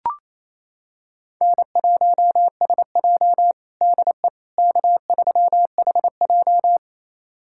Morse code, heard as a keyed tone, about 28 wpm, 700 Hz. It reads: N1SJ DE K3HJ
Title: USA Callsign DE Callsign - 4 Characters - 28wpm